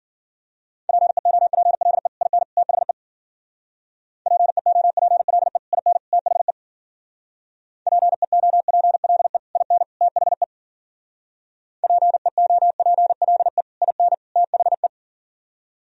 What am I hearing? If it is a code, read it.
Morse code, heard as a keyed tone, about 49 wpm, 700 Hz. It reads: PEOPLE IN THE PEOPLE IN THE PEOPLE IN THE PEOPLE IN THE
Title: Speed-Racing - Sets of 3 Words 40wpm